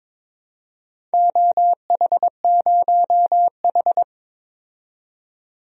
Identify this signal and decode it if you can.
Morse code, heard as a keyed tone, about 21 wpm, 700 Hz. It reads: OH0H
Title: DXCC-100 Call Signs - 4 Characters - 22wpm